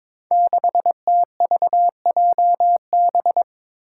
Morse code, heard as a keyed tone, about 22 wpm, 700 Hz. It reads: 6TVJB
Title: Five Challenging Letter-Number Combinations 22wpm